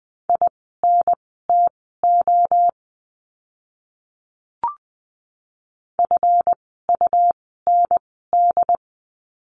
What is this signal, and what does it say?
Morse code, heard as a keyed tone, about 20 wpm, spaced as for 14 wpm, 700 Hz. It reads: INTO FUND
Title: 139 Lesson 23f - Letter f - Words 20wpm